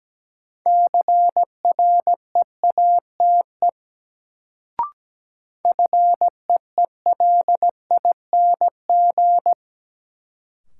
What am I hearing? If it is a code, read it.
Morse code, heard as a keyed tone, about 17 wpm, 700 Hz. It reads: CREATE FEELING
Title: Sentences from Top 1000 Words - Review Part 3 - New Word Drill - 17wpm